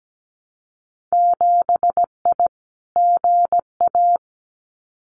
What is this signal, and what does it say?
Morse code, heard as a keyed tone, about 17 wpm, 700 Hz. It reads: 7I GA
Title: Winter Field Day Exchanges Encore 17wpm